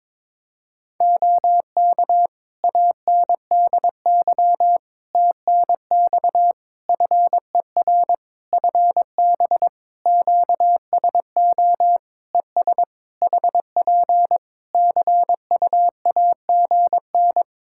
Morse code, heard as a keyed tone, about 22 wpm, 700 Hz. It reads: OK ANDY TNX FER FB QSO ES HP CUAGN
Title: K1USN SST QSO Practice 22wpm